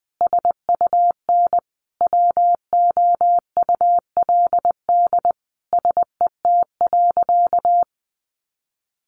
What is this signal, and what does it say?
Morse code, heard as a keyed tone, about 20 wpm, 700 Hz. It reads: SUN WOULD SET.
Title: Sentences from Top 400 Words 20wpm